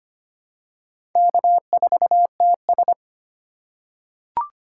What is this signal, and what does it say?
Morse code, heard as a keyed tone, about 25 wpm, 700 Hz. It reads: K4TS
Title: USA Call Signs - 2x Character Spacing - 25wpm